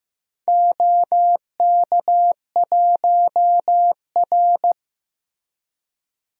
Morse code, heard as a keyed tone, about 15 wpm, 700 Hz. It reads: OK1R
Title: DXCC-100 Call Signs - 4 Characters - Repeated 2x - 15wpm